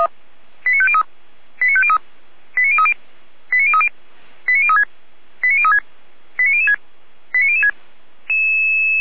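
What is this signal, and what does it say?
Five-tone selective call (ZVEI), heard as quick runs of five tones, tones of about 70 ms. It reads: 89503 89503 89030 89030 89036 89036 89006 89006
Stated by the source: ZVEI five-tone selective call 89001 received by SDR, from the feuerwehr-orlamuende fire brigade audio server.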